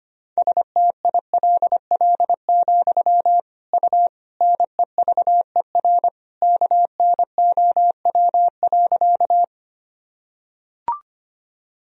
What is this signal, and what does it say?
Morse code, heard as a keyed tone, about 25 wpm, 700 Hz. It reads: STILL, U NEVER KNOW.
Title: Sentences from Top 300 Words and Abbreviations 25wpm